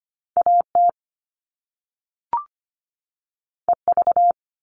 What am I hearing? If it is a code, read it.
Morse code, heard as a keyed tone, about 25 wpm, 700 Hz. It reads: AT EV
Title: All Cut Numbers - 2-Digit 25wpm